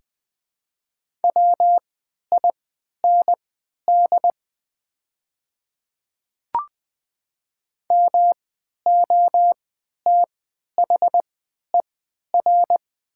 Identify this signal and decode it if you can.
Morse code, heard as a keyed tone, about 20 wpm, spaced as for 11 wpm, 700 Hz. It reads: WIND MOTHER
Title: 3x Character Spacing - Top 500 Words - 20wpm